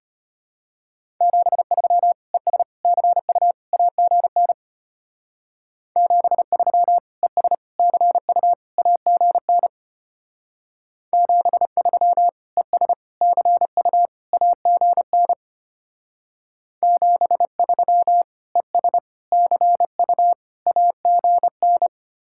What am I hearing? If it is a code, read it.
Morse code, heard as a keyed tone, about 32 wpm, 700 Hz. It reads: 73 ES CU AGN 73 ES CU AGN 73 ES CU AGN 73 ES CU AGN